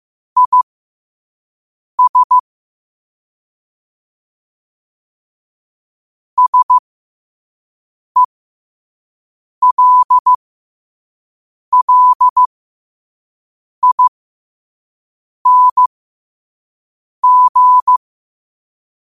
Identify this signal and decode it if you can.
Morse code, heard as a keyed tone, about 15 wpm, spaced as for 5 wpm, 1000 Hz. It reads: IS SELLING
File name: qotdfiles / 5wpm